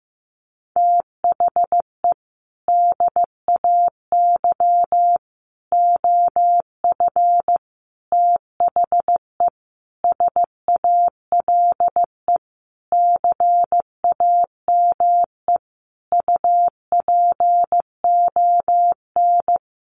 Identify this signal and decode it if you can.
Morse code, heard as a keyed tone, about 15 wpm, 700 Hz. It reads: THE DAY OF THE SALE CAME UPON
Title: Sentences from Top 1400 Words 15wpm